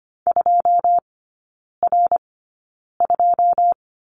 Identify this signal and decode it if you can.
Morse code, heard as a keyed tone, about 25 wpm, spaced as for 9 wpm, 700 Hz. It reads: 2R2
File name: Lesson 14e - Number 2 - Three Characters 25wpm